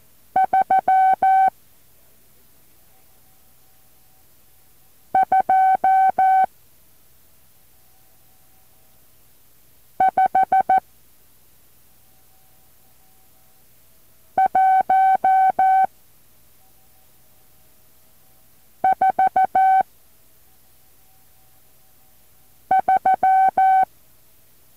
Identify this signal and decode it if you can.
Morse code, heard as a keyed tone, about 14 wpm, 760 Hz. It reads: 3 2 5 1 4 3
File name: intl morse code2